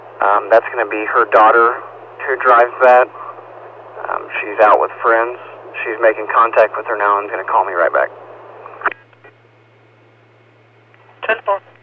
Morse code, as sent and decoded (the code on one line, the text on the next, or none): none